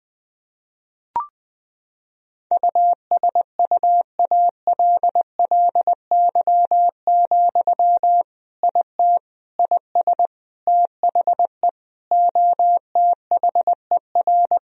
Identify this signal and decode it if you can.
Morse code, heard as a keyed tone, about 20 wpm, 700 Hz. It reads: USUALLY, IT IS THE OTHER
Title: Sentences from Top 1000 Words - Review Part 3 - 20wpm